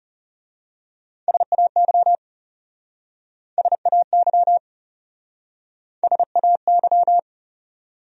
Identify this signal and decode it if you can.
Morse code, heard as a keyed tone, about 37 wpm, 700 Hz. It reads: SAY SAY SAY
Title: Speed-Racing Top 200 Words 30wpm